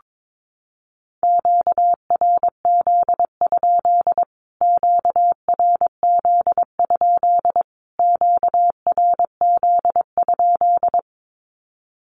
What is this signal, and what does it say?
Morse code, heard as a keyed tone, about 22 wpm, 700 Hz. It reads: QRZ? QRZ? QRZ?
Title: Q-Codes Repeated 3x 22wpm